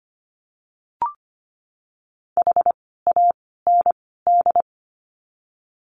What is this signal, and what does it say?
Morse code, heard as a keyed tone, about 25 wpm, 700 Hz. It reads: H A N D
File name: course / 25:10wpm / 094 Lesson 16f - Letter c - Words 25wpm